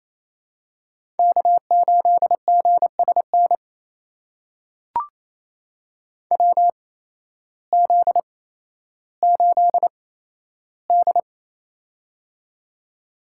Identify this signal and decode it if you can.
Morse code, heard as a keyed tone, about 28 wpm, spaced as for 13 wpm, 700 Hz. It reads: K8GSN WZ8D
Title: USA Call Signs - 8x Character Spacing - 28wpm